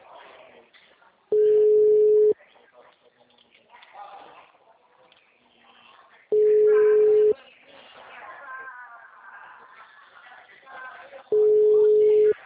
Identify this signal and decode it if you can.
Ringback tone (Europe, 425 Hz), with speech.
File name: out- -2007- - - .515